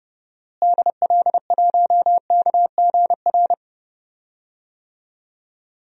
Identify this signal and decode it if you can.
Morse code, heard as a keyed tone, about 30 wpm, 700 Hz. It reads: DL1KGR
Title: DXCC-100 Call Signs - 6 Characters - Repeated 3x - 30wpm